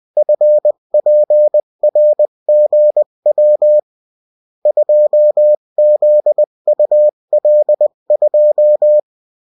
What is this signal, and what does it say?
Morse code, heard as a keyed tone, about 20 wpm, 600 Hz. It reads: FPRGW 2ZUL2
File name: morse mp3 / revision9-vitesse 18 mots